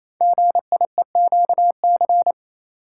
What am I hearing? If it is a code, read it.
Morse code, heard as a keyed tone, about 27 wpm, 700 Hz. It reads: GIEQC
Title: Code Groups - 5 Letter:Numbers - 28wpm